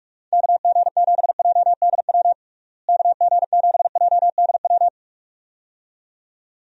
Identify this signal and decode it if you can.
Morse code, heard as a keyed tone, about 46 wpm, 700 Hz. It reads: KG7JDW KG7JDW
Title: USA Call Signs - 6 Characters - Repeated 2x - 45wpm